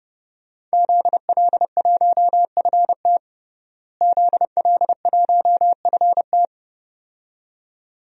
Morse code, heard as a keyed tone, about 31 wpm, 700 Hz. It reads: ZL1FT ZL1FT
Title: DXCC-100 Call Signs - 5 Characters - Repeated 2x - 30wpm